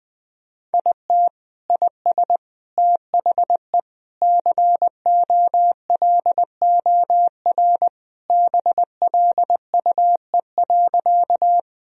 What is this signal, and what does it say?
Morse code, heard as a keyed tone, about 20 wpm, 700 Hz. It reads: IT IS THE COLOR BLUE.
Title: Sentences from Top 1000 Words - Review Part 3 - 20wpm